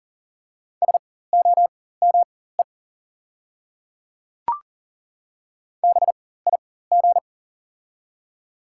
Morse code, heard as a keyed tone, about 40 wpm, spaced as for 19 wpm, 700 Hz. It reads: SOME BIG
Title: 4x Character Spacing - Top 300 Words - 40wpm